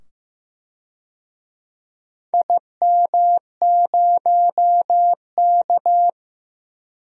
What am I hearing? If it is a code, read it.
Morse code, heard as a keyed tone, about 15 wpm, 700 Hz. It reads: IM0K
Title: DXCC-100 Call Signs - 4 Characters - 15wpm